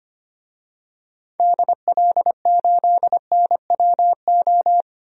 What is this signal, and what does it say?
Morse code, heard as a keyed tone, about 25 wpm, 700 Hz. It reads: DL8NWO
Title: DXCC-100 Call Signs - 6 Characters - 25wpm